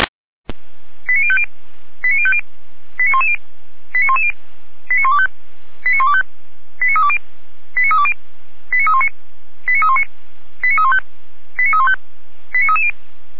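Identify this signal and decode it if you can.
Five-tone selective call (ZVEI), heard as quick runs of five tones, tones of about 80 ms. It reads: 89050 89050 89110 89110 89125 89125 89320 89320 89319 89319 89315 89315 89330
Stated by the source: ZVEI five-tone selective call received with an SDR.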